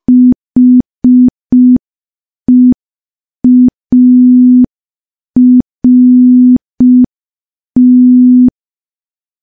Morse code, heard as a keyed tone, about 5 wpm, 250 Hz. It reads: HEART
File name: morse / morse-training-05WPM